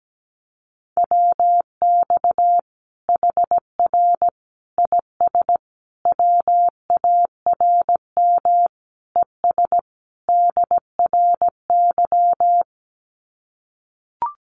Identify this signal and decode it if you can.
Morse code, heard as a keyed tone, about 17 wpm, 700 Hz. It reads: WX HR IS WARM ES DRY
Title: Common QSO Phrases 17wpm